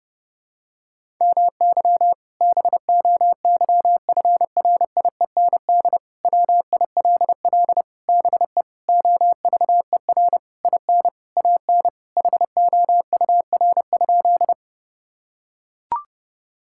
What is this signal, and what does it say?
Morse code, heard as a keyed tone, about 32 wpm, 700 Hz. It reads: MY BOYFRIEND WILL BE OVER IN AN HOUR?
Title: Sentences from Top 1000 Words - Review Part 5 - 30wpm